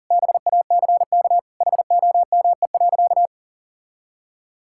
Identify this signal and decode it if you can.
Morse code, heard as a keyed tone, about 37 wpm, 700 Hz. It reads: BACK HOME.
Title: Sentences from Top 1000 Words - Review Part 6 - 40wpm